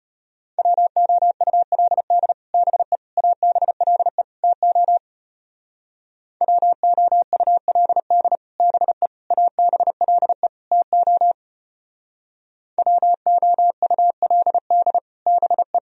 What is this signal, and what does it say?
Morse code, heard as a keyed tone, about 31 wpm, 700 Hz. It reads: WOULD BE ABLE TO WOULD BE ABLE TO WOULD BE
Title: Speed-Racing - Sets of 4 Words 25wpm